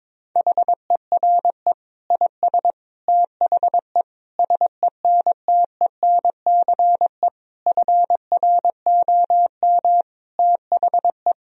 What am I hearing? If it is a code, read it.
Morse code, heard as a keyed tone, about 22 wpm, 700 Hz. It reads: HERE IS THE SENTENCE FROM THE
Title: Sentences from Top 1000 Words - Review Part 1 - 22wpm